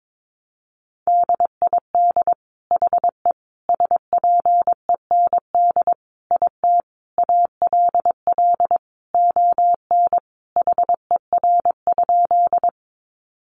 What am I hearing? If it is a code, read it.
Morse code, heard as a keyed tone, about 22 wpm, 700 Hz. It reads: DID HE SPEND IT ALL ON HER?
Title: Sentences from Top 800 Words 22wpm